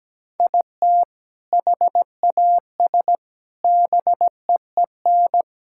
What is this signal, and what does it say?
Morse code, heard as a keyed tone, about 17 wpm, 700 Hz. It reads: IT HAS BEEN
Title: Sentences from Top 1600 Words 17wpm